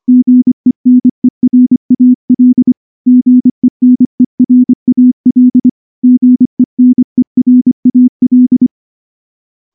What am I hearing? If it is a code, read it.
Morse code, heard as a keyed tone, about 24 wpm, 250 Hz. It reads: GENERAL GENERAL GENERAL
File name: morse / morse-training-25WPM